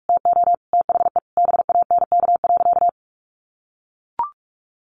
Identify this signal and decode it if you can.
Morse code, heard as a keyed tone, about 44 wpm, 700 Hz. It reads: TO THE BANK.
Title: Sentences from Top 1500 Words 45wpm